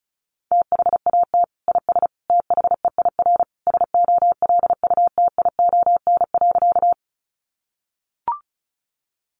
Morse code, heard as a keyed tone, about 34 wpm, 700 Hz. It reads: THAT IS THEIR SOLUTION.
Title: Sentences from Top 1300 Words 35wpm